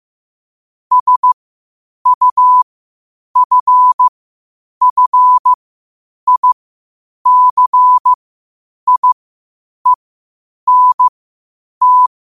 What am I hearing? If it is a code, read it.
Morse code, heard as a keyed tone, about 15 wpm, spaced as for 8 wpm, 1000 Hz. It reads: SUFFICIENT
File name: qotdfiles / 8wpm